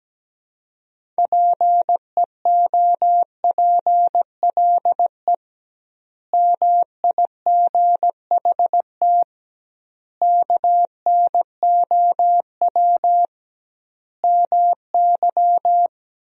Morse code, heard as a keyed tone, about 17 wpm, 700 Hz. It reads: PEOPLE MIGHT KNOW MY
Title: 2x Word-Spacing - Sentences from Top 500 Words - 17wpm